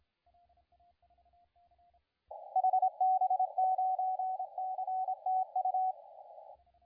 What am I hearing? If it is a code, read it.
Morse code, heard as a keyed tone, about 26 wpm, 750 Hz, with radio noise. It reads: HB9CTU
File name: la4c / sac-cw-2017 / CONTEST AUDIO 2100Z